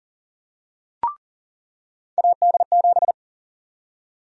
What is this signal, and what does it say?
Morse code, heard as a keyed tone, about 37 wpm, 700 Hz. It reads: AD7
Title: USA Call Sign Prefixes - 40wpm